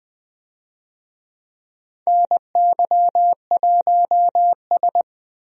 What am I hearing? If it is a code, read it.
Morse code, heard as a keyed tone, about 21 wpm, 700 Hz. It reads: NY1S
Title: USA Call Signs - 4 Characters - Repeated 2x - 20wpm